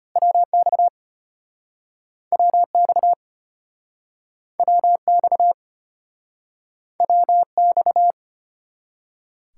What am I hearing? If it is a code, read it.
Morse code, heard as a keyed tone, about 35 wpm, 700 Hz. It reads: WX WX WX WX